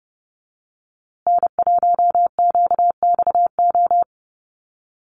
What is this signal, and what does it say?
Morse code, heard as a keyed tone, about 30 wpm, 700 Hz. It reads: N1QXO
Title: USA Call Signs Repeated 2x 30wpm